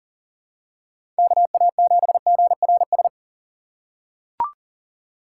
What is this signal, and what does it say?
Morse code, heard as a keyed tone, about 40 wpm, 700 Hz. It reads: KA7GRS
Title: USA Call Signs - 6 Characters - 3x Character Spacing - 40wpm